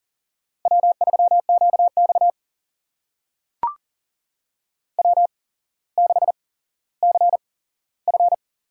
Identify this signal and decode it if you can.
Morse code, heard as a keyed tone, about 39 wpm, spaced as for 17 wpm, 700 Hz. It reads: W3QX W6CF
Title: USA Call Signs - 4 Characters - 8x Character Spacing - 40wpm